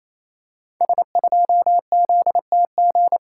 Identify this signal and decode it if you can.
Morse code, heard as a keyed tone, about 28 wpm, 700 Hz. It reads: S2ZTG
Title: Code Groups - 5 Letter:Numbers - 28wpm